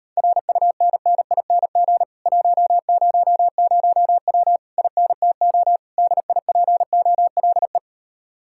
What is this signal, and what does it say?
Morse code, heard as a keyed tone, about 38 wpm, 700 Hz. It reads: RUNNING 100W INTO DIPOLE